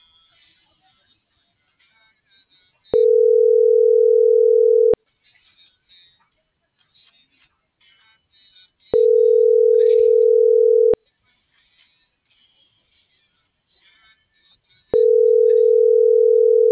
Ringback tone (North America), 440 + 480 Hz.